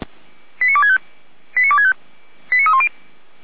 Five-tone selective call (ZVEI), heard as quick runs of five tones, tones of about 70 ms. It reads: 89265 89265 89310
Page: /455/SDR/ZVEI/